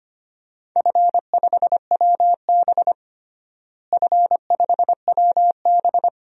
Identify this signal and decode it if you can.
Morse code, heard as a keyed tone, about 25 wpm, 700 Hz. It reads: F5WB F5WB
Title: DXCC-100 Call Signs - 4 Characters - Repeated 2x - 25wpm